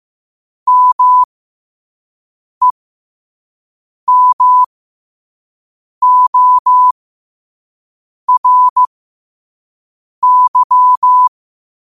Morse code, heard as a keyed tone, about 15 wpm, spaced as for 5 wpm, 1000 Hz. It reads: MEMORY
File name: qotdfiles / 5wpm